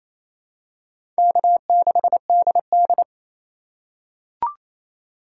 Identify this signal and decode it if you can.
Morse code, heard as a keyed tone, about 28 wpm, 700 Hz. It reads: K6DD
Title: USA Call Signs - 4 Characters - 4x Character Spacing - 28wpm